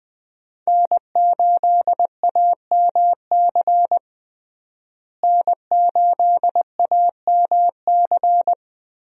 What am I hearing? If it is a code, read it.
Morse code, heard as a keyed tone, about 20 wpm, 700 Hz. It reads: N8AMC N8AMC
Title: USA Call Signs - 5 Characters - Repeated 3x - 20wpm